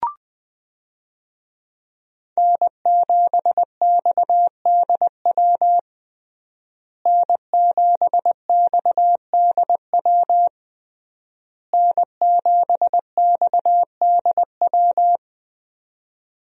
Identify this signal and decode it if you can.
Morse code, heard as a keyed tone, about 20 wpm, 700 Hz. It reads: N7XDW N7XDW N7XDW
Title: USA Call Signs - 5 Characters - Repeated 3x - 20wpm